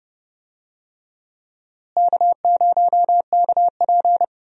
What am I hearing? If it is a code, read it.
Morse code, heard as a keyed tone, about 30 wpm, 700 Hz. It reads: K0KP
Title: USA Call Signs - 4 Characters - Repeated 3x - 30wpm